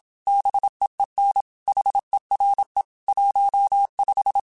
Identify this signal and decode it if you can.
Morse code, heard as a keyed tone, about 27 wpm, 800 Hz. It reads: BEEN HERE 15
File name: QSOs / QSOs-mp3s / WD0FBT 1@25